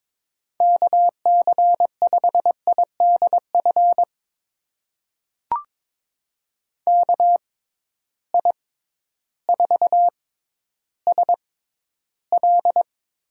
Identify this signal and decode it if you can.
Morse code, heard as a keyed tone, about 22 wpm, spaced as for 12 wpm, 700 Hz. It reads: KC5IDF KI4SL
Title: USA Call Signs - 6 Characters - 6x Character Spacing - 22wpm